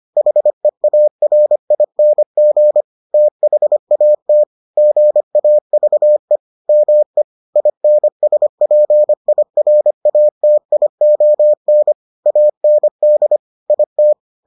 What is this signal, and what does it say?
Morse code, heard as a keyed tone, about 25 wpm, 600 Hz. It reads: HEARING THAT GAVE ME INSPIRATION AND IT